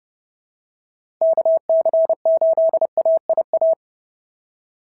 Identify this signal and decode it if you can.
Morse code, heard as a keyed tone, about 30 wpm, 690 Hz. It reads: KC8AIA
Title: SOTA QSO Practice 30wpm